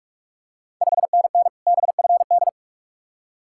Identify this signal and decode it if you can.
Morse code, heard as a keyed tone, about 44 wpm, 690 Hz. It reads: 5NN BFD
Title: Ontario QSO Party Exchange 45wpm